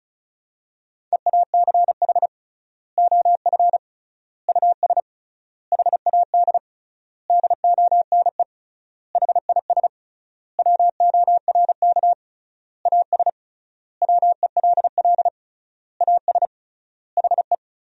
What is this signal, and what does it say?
Morse code, heard as a keyed tone, about 36 wpm, 700 Hz. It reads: EACH OF US HAD DONE HIS WORK AS WELL AS HE
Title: 3x Word-Spacing - Sentences from Top 500 Words - 35wpm